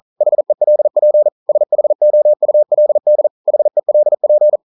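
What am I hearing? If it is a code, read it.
Morse code, heard as a keyed tone, about 42 wpm, 600 Hz. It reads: HELP SHOULD HELP
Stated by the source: Morse code practice at 41 wpm, from CW med Gnister.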